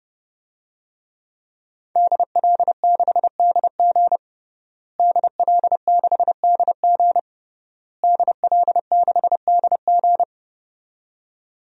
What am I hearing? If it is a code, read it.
Morse code, heard as a keyed tone, about 30 wpm, 700 Hz. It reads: DL6DG DL6DG DL6DG
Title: DXCC-100 Call Signs - 5 Characters - Repeated 3x - 30wpm